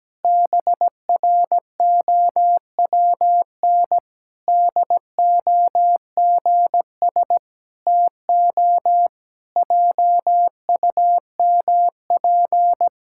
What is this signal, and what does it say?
Morse code, heard as a keyed tone, about 17 wpm, 700 Hz. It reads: BROWN DOGS TO JUMP
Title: Pangram Challenge v2 - 17wpm